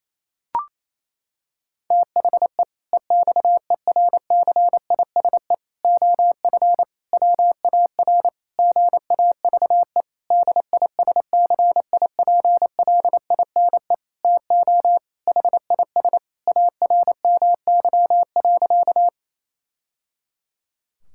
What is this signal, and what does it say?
Morse code, heard as a keyed tone, about 28 wpm, 700 Hz. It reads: THE EXERCISE OF WAR GAVE DISCIPLINE TO HIS ARMY.
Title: Sentences from Top 1000 Words - Review Part 6 - 28wpm